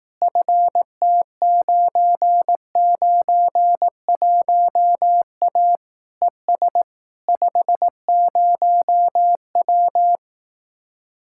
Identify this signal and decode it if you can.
Morse code, heard as a keyed tone, about 18 wpm, 700 Hz. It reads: FT991A ES 50W
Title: Speed-Racing Common QSO Phrases Encore - 15wpm